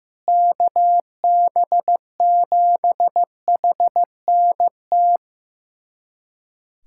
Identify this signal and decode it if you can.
Morse code, heard as a keyed tone, about 15 wpm, 700 Hz. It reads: KB7HNT
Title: USA Callsign DE Callsign - 6 Characters - 15wpm